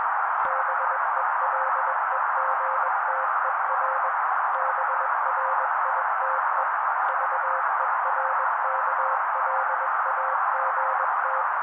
Morse code, heard as a keyed tone, about 21 wpm, 530 Hz. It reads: BELEGTER BRITE VERKLAGT
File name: daily / 20wpm noise